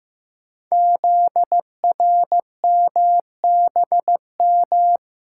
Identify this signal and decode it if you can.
Morse code, heard as a keyed tone, about 15 wpm, 690 Hz. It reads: ZRMBM
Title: Code Groups - 5 Letters - 15wpm